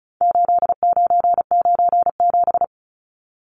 Morse code, heard as a keyed tone, about 34 wpm, 700 Hz. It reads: 8997
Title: Numbers - 4-Digit 35wpm